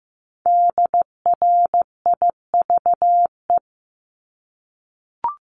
Morse code, heard as a keyed tone, about 15 wpm, 700 Hz. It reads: DRIVE
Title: Multi-letter Phonemes Repeated 3x 15wpm